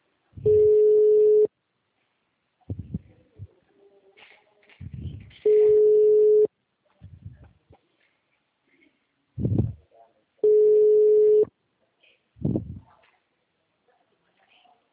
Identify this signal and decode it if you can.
Ringback tone (Europe, 425 Hz), with speech.